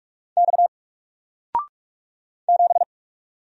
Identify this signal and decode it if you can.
Morse code, heard as a keyed tone, about 46 wpm, 700 Hz. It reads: X 7
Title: Single Letter-Number 45wpm